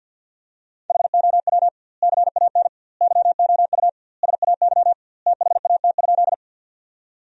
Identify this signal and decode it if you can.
Morse code, heard as a keyed tone, about 50 wpm, 700 Hz. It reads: HOW CAN YOU SAY THAT?
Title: 2x Word-Spacing - Sentences from Top 200 Words - 50wpm